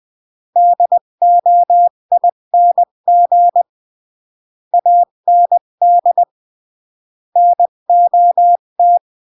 Morse code, heard as a keyed tone, about 20 wpm, 700 Hz. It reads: DOING AND NOT
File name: CW / jodorowsky / 82 commandments 20-15